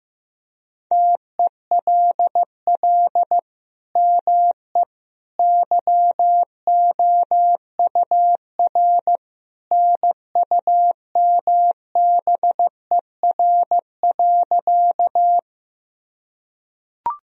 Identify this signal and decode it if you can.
Morse code, heard as a keyed tone, about 15 wpm, 700 Hz. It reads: TELL ME YOUR NUMBER.
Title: Sentences from Top 1000 Words - Review Part 1 - 15wpm